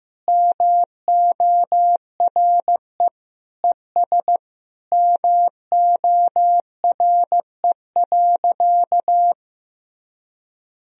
Morse code, heard as a keyed tone, about 16 wpm, 700 Hz. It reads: MORE ES MORE.
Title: Sentences from Top 500 Words and Abbreviations 15wpm